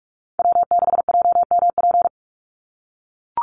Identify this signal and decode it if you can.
Morse code, heard as a keyed tone, about 44 wpm, 700 Hz. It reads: W6JMP
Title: NA SOTA Chaser Callsigns 45wpm